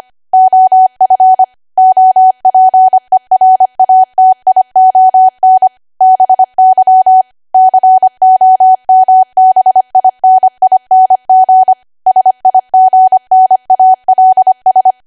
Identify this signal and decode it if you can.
Morse code, heard as a keyed tone, about 25 wpm, 750 Hz. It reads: OF OPERATION BY COMBINING SIGNALS
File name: Archive / 25 WPM / 25WPM